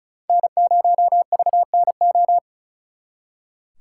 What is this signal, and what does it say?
Morse code, heard as a keyed tone, about 45 wpm, 700 Hz. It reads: N0VNO